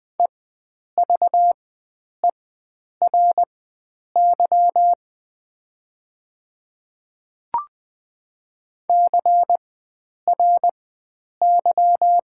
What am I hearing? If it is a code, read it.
Morse code, heard as a keyed tone, about 20 wpm, spaced as for 9 wpm, 700 Hz. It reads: EVERY CRY
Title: 4x Character Spacing - Top 500 Words - 20wpm